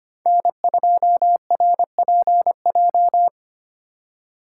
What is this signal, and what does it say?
Morse code, heard as a keyed tone, about 25 wpm, 700 Hz. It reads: N2RPJ
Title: USA Call Signs - 5 Characters - 2x Character Spacing - 25wpm